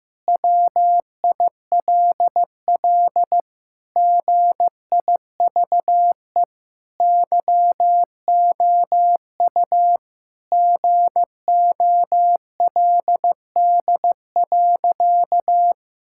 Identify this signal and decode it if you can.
Morse code, heard as a keyed tone, about 15 wpm, 700 Hz. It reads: WILL GIVE YOU GOLD.
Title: Sentences from Top 1000 Words - Review Part 6 - 15wpm